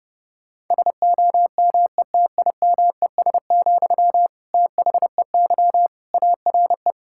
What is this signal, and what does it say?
Morse code, heard as a keyed tone, about 32 wpm, 700 Hz. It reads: SOMETIMES, THEY ARE
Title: Sentences from Top 1000 Words - Review Part 5 - 30wpm